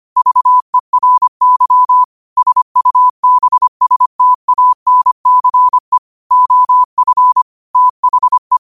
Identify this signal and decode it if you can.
Morse code, heard as a keyed tone, about 25 wpm, 1000 Hz. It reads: VERY SUBSTANCE OF THE